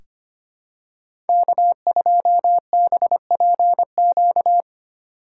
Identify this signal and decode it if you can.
Morse code, heard as a keyed tone, about 25 wpm, 700 Hz. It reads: K2BPQ
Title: USA Call Signs - 5 Characters - 2x Character Spacing - 25wpm